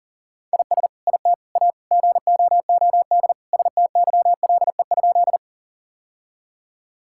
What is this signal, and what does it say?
Morse code, heard as a keyed tone, about 40 wpm, 700 Hz. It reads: IS IT A GOOD STYLE?
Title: Sentences from Top 1000 Words - Review Part 5 - 40wpm